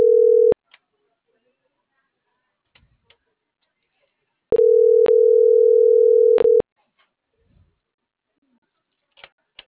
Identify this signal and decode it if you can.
Ringback tone (North America), 440 + 480 Hz.